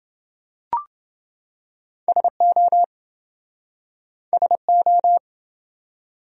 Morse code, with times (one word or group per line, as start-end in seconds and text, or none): 2.1-2.8 SO
4.3-5.2 SO